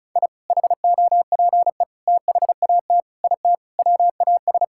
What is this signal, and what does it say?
Morse code, heard as a keyed tone, about 34 wpm, 700 Hz. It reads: I HOPE THAT IT WAS
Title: Sentences from Top 1000 Words - Review Part 5 - 35wpm